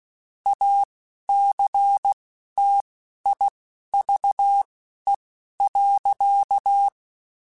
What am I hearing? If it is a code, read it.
Morse code, heard as a keyed tone, about 16 wpm, spaced as for 11 wpm, 800 Hz. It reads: ACTIVE.
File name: Technical Subjects / Technical Subjects-mp3s / OR Gate @10